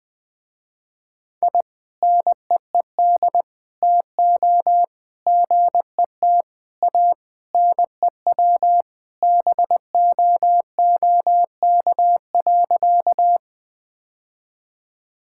Morse code, heard as a keyed tone, about 19 wpm, 700 Hz. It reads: I NEED TO GET A NEW BOOK.
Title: Sentences with 3-Word Phrases - Spoken Beginning - 20wpm